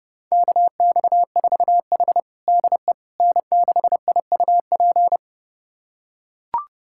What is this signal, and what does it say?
Morse code, heard as a keyed tone, about 30 wpm, 700 Hz. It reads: KX4H DE N6IUP
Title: USA Callsign DE Callsign 30wpm